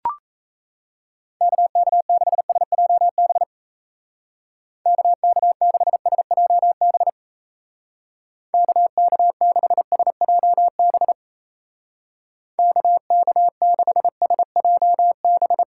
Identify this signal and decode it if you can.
Morse code, heard as a keyed tone, about 36 wpm, 700 Hz. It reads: KK6SJB KK6SJB KK6SJB KK6SJB